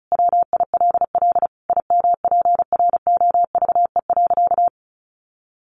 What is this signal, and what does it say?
Morse code, heard as a keyed tone, about 34 wpm, 700 Hz. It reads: WILL IMPROVE.
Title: Sentences from Top 800 Words 35wpm